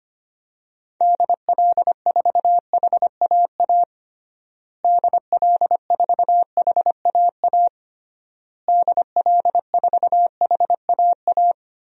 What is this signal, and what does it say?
Morse code, heard as a keyed tone, about 25 wpm, 700 Hz. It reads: DL4HAA DL4HAA DL4HAA
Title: DXCC-100 Call Signs - 6 Characters - Repeated 3x - 25wpm